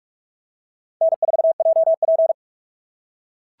Morse code, heard as a keyed tone, about 44 wpm, 690 Hz. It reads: N4JP
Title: SOTA QSO Practice 45wpm